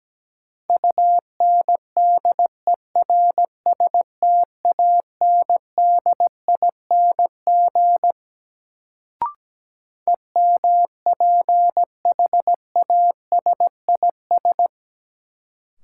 Morse code, heard as a keyed tone, about 17 wpm, 700 Hz. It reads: UNDERSTANDING EMPHASIS
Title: Sentences from Top 1400 Words - New Word Drill Alt - 17wpm